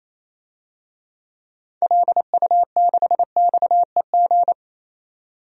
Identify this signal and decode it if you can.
Morse code, heard as a keyed tone, about 27 wpm, 700 Hz. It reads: LU6XEG
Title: DXCC-100 Call Signs - 6 Characters - 28wpm